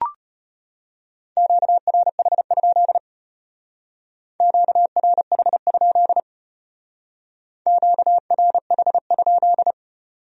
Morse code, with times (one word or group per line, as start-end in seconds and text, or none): 1.4-3.0 QRH?
4.4-6.2 QRH?
7.7-9.7 QRH?